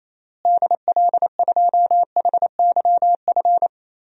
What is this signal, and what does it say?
Morse code, heard as a keyed tone, about 28 wpm, 700 Hz. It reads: DL2HYF
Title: DXCC-100 Call Signs - 6 Characters - Repeated 2x - 28wpm